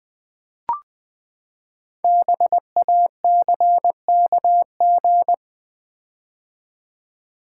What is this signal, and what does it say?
Morse code, heard as a keyed tone, about 19 wpm, 700 Hz. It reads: BACKG
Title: Build A Word - 5 Letters At A Time - 10-Letter Words - 20wpm